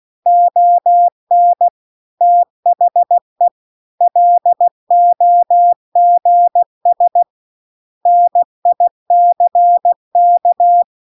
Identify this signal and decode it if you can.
Morse code, heard as a keyed tone, about 16 wpm, 700 Hz. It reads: ON THE LOGS NICK
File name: CW-The-Big-Twoheaded-River-2-16WPM